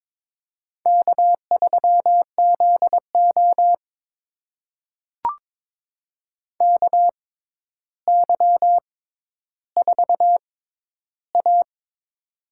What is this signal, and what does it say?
Morse code, heard as a keyed tone, about 22 wpm, spaced as for 11 wpm, 700 Hz. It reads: K3ZO KY4A
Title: USA Call Signs - 4 Characters - 6x Character Spacing - 22wpm